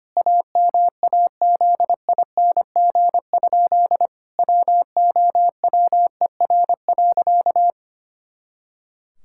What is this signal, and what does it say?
Morse code, heard as a keyed tone, about 25 wpm, 700 Hz. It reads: AMAZING? WOWER.
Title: Pangram Challenge v2 - 25wpm